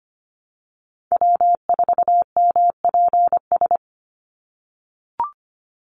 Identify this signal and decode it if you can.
Morse code, heard as a keyed tone, about 25 wpm, 700 Hz. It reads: W4MPS
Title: NA SOTA Chaser Callsigns 25wpm